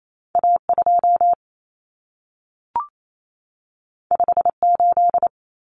Morse code, heard as a keyed tone, about 28 wpm, 700 Hz. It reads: A2 58
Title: Common Cut Numbers - 2-Digit 28wpm